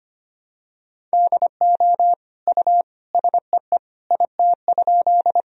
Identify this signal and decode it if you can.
Morse code, heard as a keyed tone, about 25 wpm, 700 Hz. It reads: DO U SEE IT?
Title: Sentences from Top 100 Words and Abbreviations 25wpm